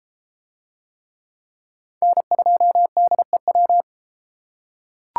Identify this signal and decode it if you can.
Morse code, heard as a keyed tone, about 29 wpm, 700 Hz. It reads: N2DEW